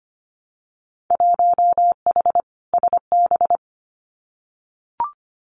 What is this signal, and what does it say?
Morse code, heard as a keyed tone, about 25 wpm, 700 Hz. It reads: 1H SB
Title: Winter Field Day Exchanges 25wpm